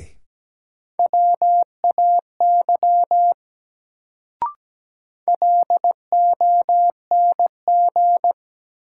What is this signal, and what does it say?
Morse code, heard as a keyed tone, about 17 wpm, 700 Hz. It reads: WAY LONG
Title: Sentences from Top 100 Words - New Word Drill - 17wpm